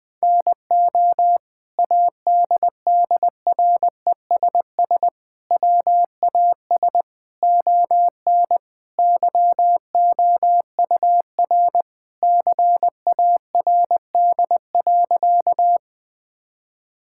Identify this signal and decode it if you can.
Morse code, heard as a keyed tone, about 21 wpm, 700 Hz. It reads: NO ADDRESS WAS ON YOUR CARD.
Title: Sentences from Top 1000 Words - Review Part 6 - 20wpm